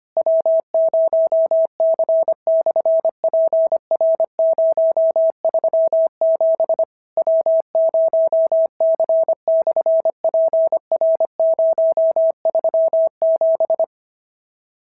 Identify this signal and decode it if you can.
Morse code, heard as a keyed tone, about 25 wpm, 650 Hz. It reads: W0C/PR037 W0C/PR037
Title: SOTA QSO Practice 25wpm